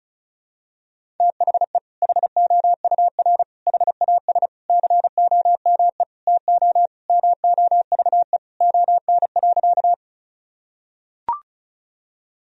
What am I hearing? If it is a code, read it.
Morse code, heard as a keyed tone, about 34 wpm, 700 Hz. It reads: THE HOUR HAS COME TO MOVE ON.
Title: Sentences from Top 1000 Words - Review Part 1 - 35wpm